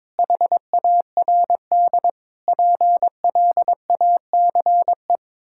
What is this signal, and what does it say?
Morse code, heard as a keyed tone, about 22 wpm, 700 Hz. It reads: HARD PLACE
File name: practice / 22wpm / Sets of 8 Words - 22wpm